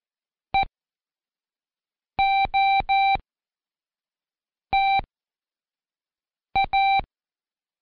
Morse code, heard as a keyed tone, about 14 wpm, spaced as for 5 wpm, 780 Hz. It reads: EOTA
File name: Code Quick / AudioDownld1 / Seg18-Day2-Quiz ASYO